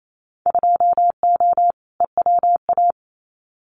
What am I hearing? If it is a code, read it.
Morse code, heard as a keyed tone, about 28 wpm, 700 Hz. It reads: 2O EWA
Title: Winter Field Day Exchanges Encore 28wpm